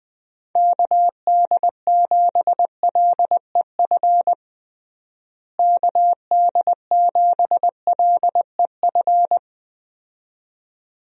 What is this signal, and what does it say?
Morse code, heard as a keyed tone, about 21 wpm, 700 Hz. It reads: KD7LEF KD7LEF
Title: USA Call Signs - 6 Characters - Repeated 2x - 20wpm